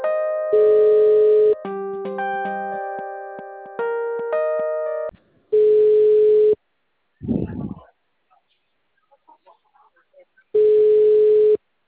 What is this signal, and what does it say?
Ringback tone (Europe, 425 Hz).
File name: out- -unknown- - - .68125